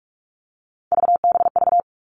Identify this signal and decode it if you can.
Morse code, heard as a keyed tone, about 44 wpm, 700 Hz. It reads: VBV
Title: All Cut Numbers - 3-Digit 45wpm